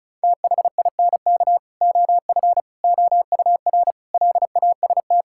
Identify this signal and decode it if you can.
Morse code, heard as a keyed tone, about 34 wpm, 700 Hz. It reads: THINK OF OUR LAST
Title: Sentences from Top 1600 Words 35wpm